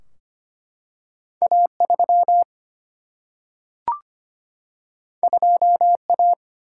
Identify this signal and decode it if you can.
Morse code, heard as a keyed tone, about 25 wpm, 700 Hz. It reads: A3 2A
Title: Single-Letter Single-Number 25wpm